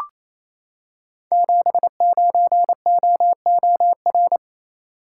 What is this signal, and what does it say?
Morse code, heard as a keyed tone, about 27 wpm, 700 Hz. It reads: 79OOR
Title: Code Groups - 5 Letter:Numbers - 28wpm